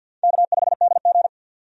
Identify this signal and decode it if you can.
Morse code, heard as a keyed tone, about 48 wpm, 700 Hz. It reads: K5DG
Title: USA Call Signs - 4 Characters - 50wpm